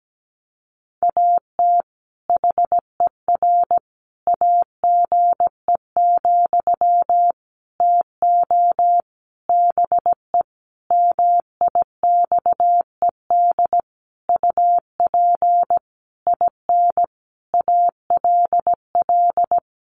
Morse code, heard as a keyed tone, about 17 wpm, 700 Hz. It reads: AT HER AGE, TO BE MIXED UP IN ALL
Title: Sentences from Top 1500 Words 17wpm